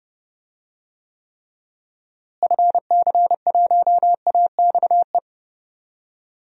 Morse code, heard as a keyed tone, about 28 wpm, 700 Hz. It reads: FC1AXE
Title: DXCC-100 Call Signs - 6 Characters - Repeated 3x - 30wpm